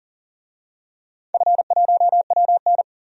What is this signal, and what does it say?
Morse code, heard as a keyed tone, about 40 wpm, 700 Hz. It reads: F1WN
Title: DXCC-100 Call Signs - 4 Characters - 40wpm